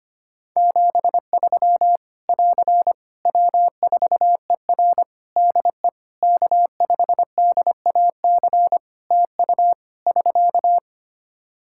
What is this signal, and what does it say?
Morse code, heard as a keyed tone, about 25 wpm, 700 Hz. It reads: 73 <AR> W4ER DE K5DAC TU <SK>
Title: K1USN SST QSO Practice 25wpm